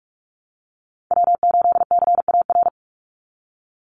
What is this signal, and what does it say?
Morse code, heard as a keyed tone, about 44 wpm, 700 Hz. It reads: W8CAR
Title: Call Signs - 45wpm